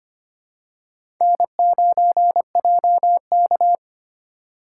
Morse code, heard as a keyed tone, about 25 wpm, 700 Hz. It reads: N9JK
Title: USA Call Signs - 4 Characters - 3x Character Spacing - 25wpm